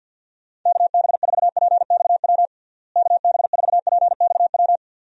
Morse code, heard as a keyed tone, about 46 wpm, 700 Hz. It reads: KB4PXW KB4PXW
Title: USA Call Signs - 6 Characters - Repeated 2x - 50wpm